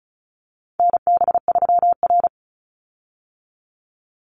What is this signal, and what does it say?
Morse code, heard as a keyed tone, about 34 wpm, 700 Hz. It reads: NB3R
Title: Call Signs - 35wpm